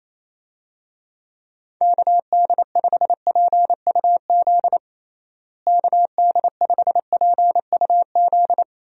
Morse code, heard as a keyed tone, about 27 wpm, 700 Hz. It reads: KD5PUZ KD5PUZ
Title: USA Call Signs - 6 Characters - Repeated 2x - 28wpm